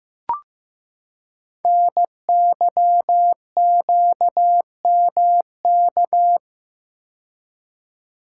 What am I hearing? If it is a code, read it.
Morse code, heard as a keyed tone, about 15 wpm, 700 Hz. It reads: NYQMK
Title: Code Groups - 5 Letters - 15wpm